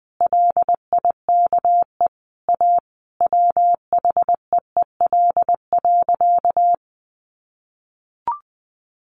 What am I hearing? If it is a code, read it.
Morse code, heard as a keyed tone, about 20 wpm, 700 Hz. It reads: LIKE A WHEEL.
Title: Sentences from Top 500 Words 20wpm